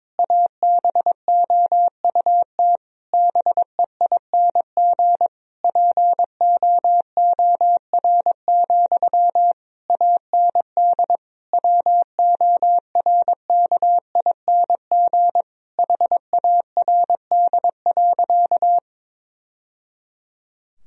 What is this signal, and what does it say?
Morse code, heard as a keyed tone, about 22 wpm, 700 Hz. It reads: ABOUT BEING POOR, AND WORKING HARD.
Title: Sentences from Top 1000 Words - Review Part 4 - 22wpm